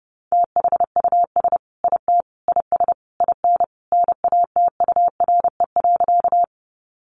Morse code, heard as a keyed tone, about 28 wpm, 700 Hz. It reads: THUS IT IS IN NATURE.
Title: Sentences from Top 800 Words 30wpm